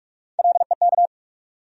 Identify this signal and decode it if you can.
Morse code, heard as a keyed tone, about 42 wpm, 700 Hz. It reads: LEK
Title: USA Call Sign Postfixes - 3 Letters 45wpm